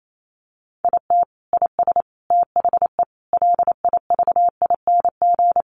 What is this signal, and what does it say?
Morse code, heard as a keyed tone, about 28 wpm, 700 Hz. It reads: IT IS THE LIVING
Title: Sentences from Top 1100 Words 28wpm